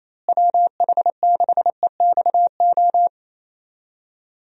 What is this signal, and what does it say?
Morse code, heard as a keyed tone, about 28 wpm, 700 Hz. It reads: WH6EXO
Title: USA Call Signs - 6 Characters - Repeated 3x - 28wpm